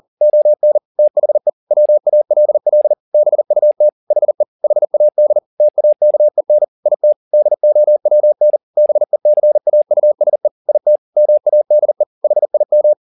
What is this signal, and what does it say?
Morse code, heard as a keyed tone, about 40 wpm, 600 Hz. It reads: ON THE WALL BUT HE HAD TAKEN IT DOWN BECAUSE IT MADE HIM